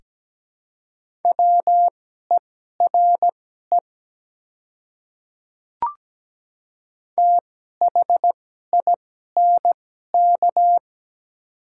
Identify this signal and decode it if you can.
Morse code, heard as a keyed tone, about 17 wpm, spaced as for 12 wpm, 700 Hz. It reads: WERE THINK
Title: 2x Character Spacing - Top 200 Words - 17wpm